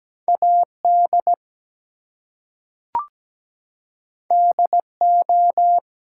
Morse code, heard as a keyed tone, about 17 wpm, 700 Hz. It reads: AD DO
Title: Two Letter Words - 17wpm